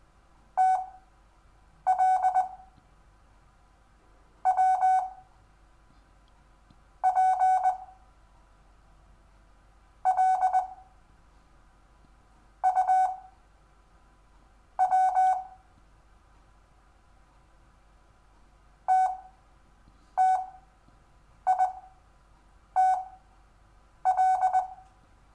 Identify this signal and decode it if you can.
Morse code, heard as a keyed tone, about 20 wpm, spaced as for 7 wpm, 750 Hz. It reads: TL W P L U W TTITL
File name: cw / Lesson6 WI